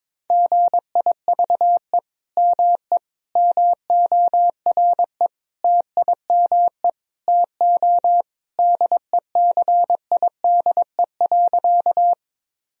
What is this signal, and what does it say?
Morse code, heard as a keyed tone, about 22 wpm, 700 Hz. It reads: GIVE ME MORE TIME TO DECIDE.
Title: Sentences from Top 1000 Words - Review Part 3 - 22wpm